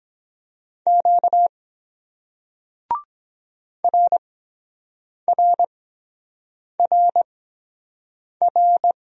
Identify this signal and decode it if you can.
Morse code, heard as a keyed tone, about 23 wpm, spaced as for 7 wpm, 700 Hz. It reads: Q RRRR